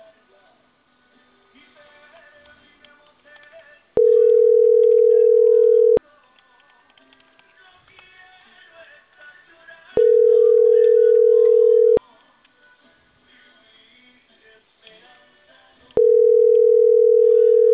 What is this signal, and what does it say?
Ringback tone (North America), 440 + 480 Hz.